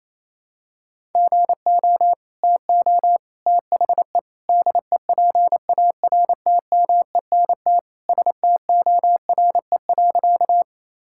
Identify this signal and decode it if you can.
Morse code, heard as a keyed tone, about 28 wpm, 700 Hz. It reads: GO TO THE DEPARTMENT STORE.
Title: Sentences from Top 1000 Words - Review Part 5 - 28wpm